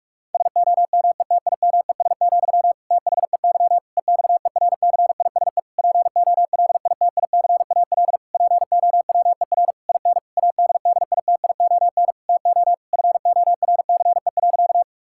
Morse code, heard as a keyed tone, about 46 wpm, 700 Hz. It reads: SOMETIMES, THEY EXERCISE POLITICAL POWER IN ADDITION TO FORCE.
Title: Sentences from Top 1000 Words - Review Part 6 - 45wpm